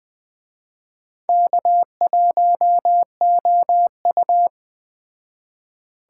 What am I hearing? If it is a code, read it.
Morse code, heard as a keyed tone, about 20 wpm, 700 Hz. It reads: K1OU